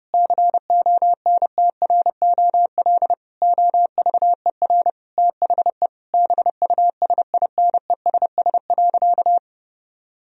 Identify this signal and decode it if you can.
Morse code, heard as a keyed tone, about 29 wpm, 700 Hz. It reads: CONTROL OVER THE BUSINESS.
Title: Sentences from Top 1000 Words - Review Part 3 - 30wpm